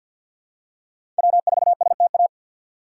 Morse code, heard as a keyed tone, about 50 wpm, 700 Hz. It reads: W4STA
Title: USA Call Signs - 5 Characters - 4x Character Spacing - 50wpm